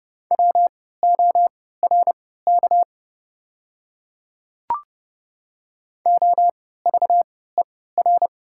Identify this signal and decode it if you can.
Morse code, heard as a keyed tone, about 30 wpm, spaced as for 17 wpm, 690 Hz. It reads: WORK OVER
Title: 3x Character Spacing - Top 200 Words - 30wpm